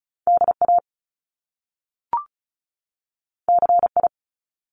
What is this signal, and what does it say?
Morse code, heard as a keyed tone, about 36 wpm, 700 Hz. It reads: DA CI
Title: Top Two Letter Combinations in Words 35wpm